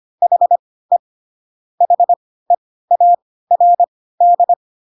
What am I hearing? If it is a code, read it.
Morse code, heard as a keyed tone, about 25 wpm, spaced as for 16 wpm, 700 Hz. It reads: HE HEARD
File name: Pinocchio / Pinocchio ch4 10 25 ik1hge